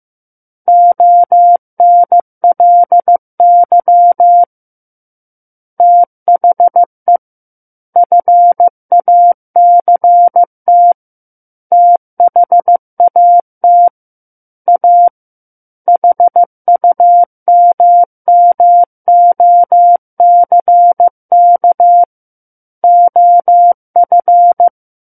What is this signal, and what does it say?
Morse code, heard as a keyed tone, about 15 wpm, 700 Hz. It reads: ONLY THE FACT THAT A HUMMOCK OF
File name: Book One / 06-Chapter 6 - 15 WPM